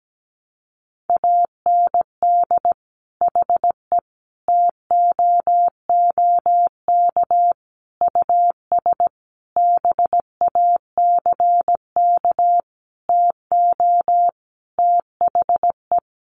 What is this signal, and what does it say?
Morse code, heard as a keyed tone, about 17 wpm, 700 Hz. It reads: AND HE TOOK US BACK TO THE
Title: Sentences from Top 1500 Words 17wpm